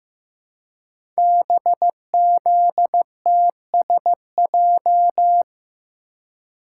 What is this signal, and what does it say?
Morse code, heard as a keyed tone, about 15 wpm, 700 Hz. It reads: BZTSJ
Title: Code Groups - 5 Letters - 15wpm